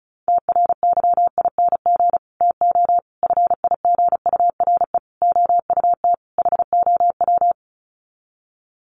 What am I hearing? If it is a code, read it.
Morse code, heard as a keyed tone, about 34 wpm, 700 Hz. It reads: TRYING TO FIGURE OUT HOW
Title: Sets of 5 Words Encore 35wpm